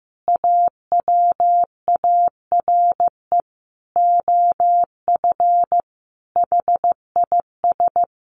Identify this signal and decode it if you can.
Morse code, heard as a keyed tone, about 15 wpm, 700 Hz. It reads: AWARE OF HIS
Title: Sentences from Top 1200 Words 15wpm